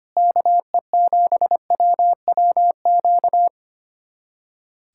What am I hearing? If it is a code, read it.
Morse code, heard as a keyed tone, about 25 wpm, 700 Hz. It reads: KE7WWQ
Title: USA Callsign DE Callsign 25wpm